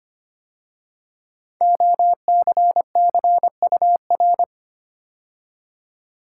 Morse code, heard as a keyed tone, about 25 wpm, 700 Hz. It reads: OCCUR
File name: 5 Letter Words - First 3 Letters Given - 25wpm